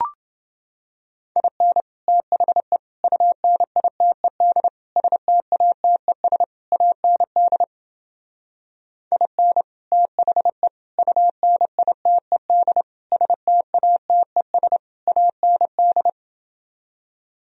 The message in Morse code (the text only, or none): IN THE UNITED STATES AND IN THE UNITED STATES AND